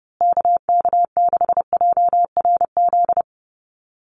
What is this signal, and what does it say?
Morse code, heard as a keyed tone, about 30 wpm, 700 Hz. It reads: KK6JRZ
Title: USA Call Signs Repeated 2x 30wpm